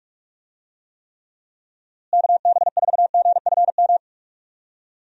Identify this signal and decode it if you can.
Morse code, heard as a keyed tone, about 46 wpm, 700 Hz. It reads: KB4GFM
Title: USA Call Signs - 6 Characters - Repeated 2x - 45wpm